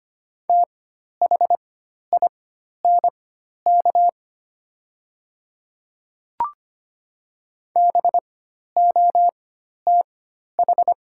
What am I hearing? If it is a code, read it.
Morse code, heard as a keyed tone, about 25 wpm, spaced as for 12 wpm, 700 Hz. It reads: THINK BOTH
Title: 4x Character Spacing - Top 300 Words - 25wpm